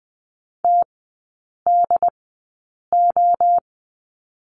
Morse code, heard as a keyed tone, about 20 wpm, spaced as for 8 wpm, 700 Hz. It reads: TDO
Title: 184 Lesson 30e - Letter b - Three Characters 20wpm